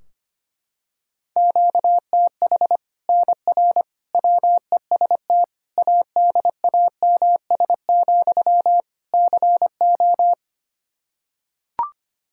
Morse code, heard as a keyed tone, about 25 wpm, 700 Hz. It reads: QTH NR WEST ADAMS, CO
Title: Practice QSO - Exchange 1 - Beginning - 25wpm